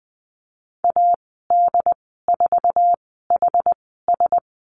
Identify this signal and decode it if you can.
Morse code, heard as a keyed tone, about 19 wpm, spaced as for 14 wpm, 700 Hz. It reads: AD4HS
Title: 088 Lesson 15g - Number 5 - Call Signs 20wpm